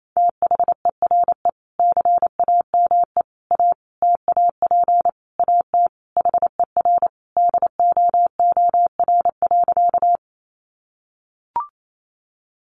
Morse code, heard as a keyed tone, about 28 wpm, 700 Hz. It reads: THERE CAME A TAP AT HER DOOR.
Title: Sentences from Top 1400 Words 28wpm